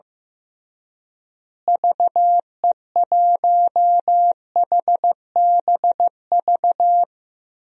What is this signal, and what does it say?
Morse code, heard as a keyed tone, about 15 wpm, 700 Hz. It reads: VE1HBV
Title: DXCC-100 Call Signs - 6 Characters - Repeated 3x - 15wpm